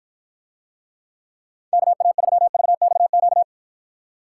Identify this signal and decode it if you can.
Morse code, heard as a keyed tone, about 53 wpm, 700 Hz. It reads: KA3VXQ